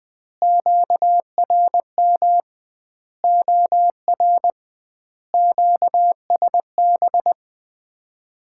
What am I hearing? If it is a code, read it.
Morse code, heard as a keyed tone, about 20 wpm, 700 Hz. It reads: QRM OR QSB
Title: Common QSO Phrases - 2x Word-Spacing - 20wpm